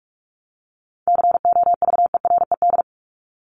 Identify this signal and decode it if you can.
Morse code, heard as a keyed tone, about 44 wpm, 700 Hz. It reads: COVERED
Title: ED Suffix - Top 100 Words with Base Words 45wpm